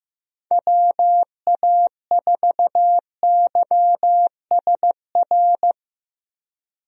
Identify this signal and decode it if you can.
Morse code, heard as a keyed tone, about 16 wpm, 700 Hz. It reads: WA4YSR
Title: USA Call Signs - 6 Characters - Repeated 3x - 15wpm